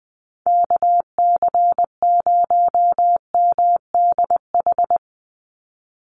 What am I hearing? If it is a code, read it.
Morse code, heard as a keyed tone, about 20 wpm, 700 Hz. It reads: KC0MDH
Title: USA Call Signs 20wpm